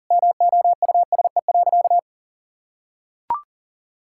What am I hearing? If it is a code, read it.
Morse code, heard as a keyed tone, about 40 wpm, 700 Hz. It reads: MOUSE.
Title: Sentences from Top 1000 Words - Review Part 5 - 40wpm